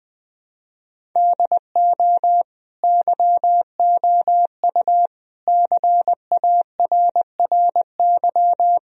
Morse code, heard as a keyed tone, about 20 wpm, 700 Hz. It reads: DO YOU CARRY
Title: Sentences from Top 1000 Words - Review Part 1 - 20wpm